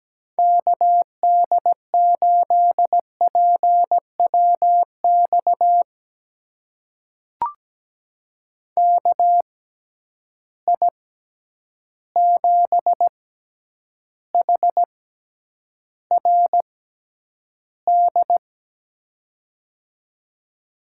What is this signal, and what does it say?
Morse code, heard as a keyed tone, about 17 wpm, spaced as for 9 wpm, 700 Hz. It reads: KD8PWX KI7HRD
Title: USA Call Signs - 6 Characters - 6x Character Spacing - 17wpm